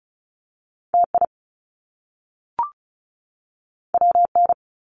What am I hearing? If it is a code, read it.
Morse code, heard as a keyed tone, about 36 wpm, 700 Hz. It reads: TI WN
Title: Top Two Letter Combinations in Words 35wpm